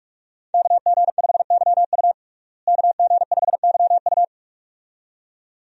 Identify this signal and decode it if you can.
Morse code, heard as a keyed tone, about 46 wpm, 700 Hz. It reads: KG5YU KG5YU
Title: USA Call Signs - 5 Characters - Repeated 2x - 45wpm